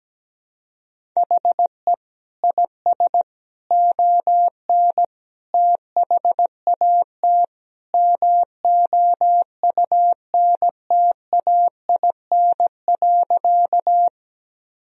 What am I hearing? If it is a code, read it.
Morse code, heard as a keyed tone, about 17 wpm, 700 Hz. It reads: HE IS ON THAT MOUNTAIN.
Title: Sentences from Top 1000 Words - Review Part 1 - 17wpm